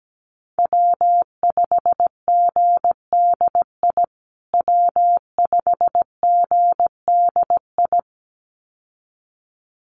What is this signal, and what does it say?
Morse code, heard as a keyed tone, about 17 wpm, 700 Hz. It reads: W5GDI W5GDI
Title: USA Call Signs Repeated 2x 17wpm